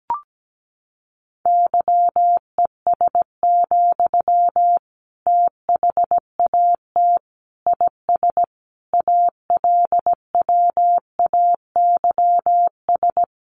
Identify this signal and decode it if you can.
Morse code, heard as a keyed tone, about 17 wpm, 700 Hz. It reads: YES, THAT IS ALWAYS
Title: Sentences from Top 1500 Words 17wpm